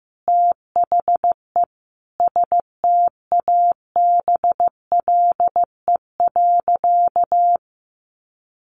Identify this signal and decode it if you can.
Morse code, heard as a keyed tone, about 15 wpm, 700 Hz. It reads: THE STABLE.
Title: Sentences from Top 1500 Words 15wpm